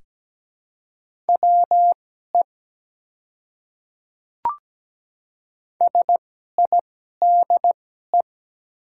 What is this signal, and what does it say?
Morse code, heard as a keyed tone, about 17 wpm, spaced as for 12 wpm, 700 Hz. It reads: WE SIDE
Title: 2x Character Spacing - Top 200 Words - 17wpm